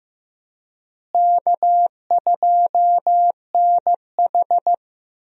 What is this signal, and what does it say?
Morse code, heard as a keyed tone, about 15 wpm, 700 Hz. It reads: K2NH
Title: USA Call Signs - 4 Characters - Repeated 3x - 15wpm